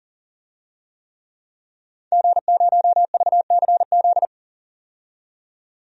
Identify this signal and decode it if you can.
Morse code, heard as a keyed tone, about 40 wpm, 700 Hz. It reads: G0VCZ
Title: DXCC-100 Call Signs - 5 Characters - 40wpm